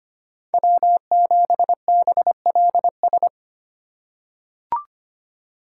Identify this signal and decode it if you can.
Morse code, heard as a keyed tone, about 25 wpm, 700 Hz. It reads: W7BLS
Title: USA Call Signs - 5 Characters - 3x Character Spacing - 25wpm